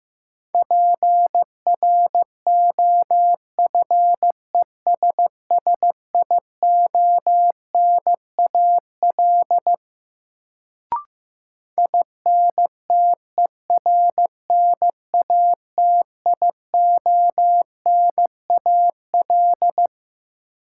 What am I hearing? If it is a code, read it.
Morse code, heard as a keyed tone, about 15 wpm, 700 Hz. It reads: PROFESSIONAL INTERNATIONAL
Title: Sentences from Top 1000 Words - Review Part 4 - New Word Drill - 15wpm